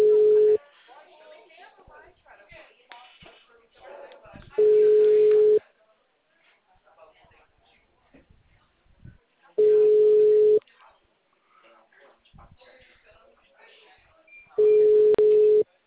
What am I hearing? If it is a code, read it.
Ringback tone (Europe, 425 Hz), with speech.
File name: out- -2007- - - .1843